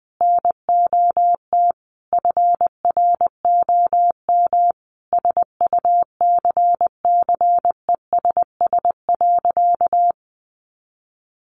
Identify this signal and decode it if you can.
Morse code, heard as a keyed tone, about 20 wpm, 700 Hz. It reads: NOT FROM SUCCESS.
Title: Sentences from Top 900 Words 20wpm